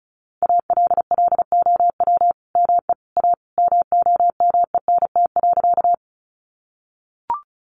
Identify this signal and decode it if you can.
Morse code, heard as a keyed tone, about 34 wpm, 700 Hz. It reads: ALLOW ME A MOMENT.
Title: Sentences from Top 1000 Words 35wpm